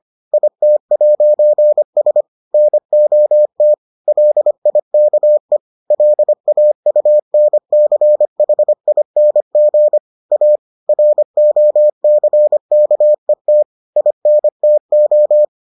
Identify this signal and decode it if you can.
Morse code, heard as a keyed tone, about 25 wpm, 600 Hz. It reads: IT'S NOT LIKE LAUNCHING A ROCKET INTO